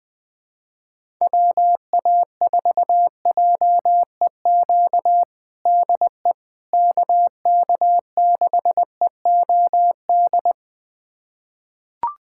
Morse code, heard as a keyed tone, about 20 wpm, 700 Hz. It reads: WA4JEQ DE KK6EOD
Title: USA Callsign DE Callsign 20wpm